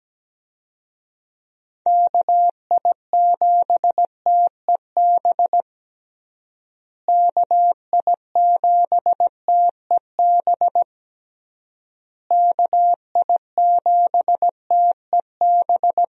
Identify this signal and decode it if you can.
Morse code, heard as a keyed tone, about 17 wpm, 700 Hz. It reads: KI7TEB KI7TEB KI7TEB
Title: USA Call Signs - 6 Characters - Repeated 3x - 17wpm